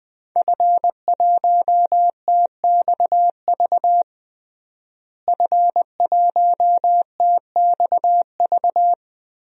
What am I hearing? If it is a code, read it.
Morse code, heard as a keyed tone, about 20 wpm, 700 Hz. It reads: F1TXV F1TXV
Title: DXCC-100 Call Signs - 5 Characters - Repeated 2x - 20wpm